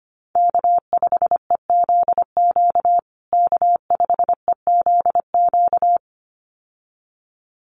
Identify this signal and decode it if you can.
Morse code, heard as a keyed tone, about 25 wpm, 700 Hz. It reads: K5EZQ K5EZQ
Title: USA Call Signs Repeated 2x 25wpm